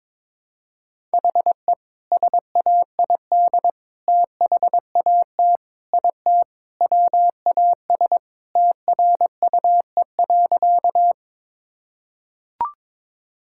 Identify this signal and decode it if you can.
Morse code, heard as a keyed tone, about 22 wpm, 700 Hz. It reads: HE SAID THAT IT WAS TRUE.
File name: practice / 22wpm / Sentences with 2-Word Phrases - 22wpm